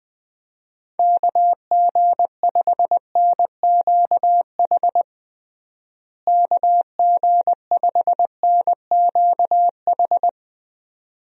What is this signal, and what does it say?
Morse code, heard as a keyed tone, about 20 wpm, 700 Hz. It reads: KG5NQH KG5NQH
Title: USA Call Signs - 6 Characters - Repeated 2x - 20wpm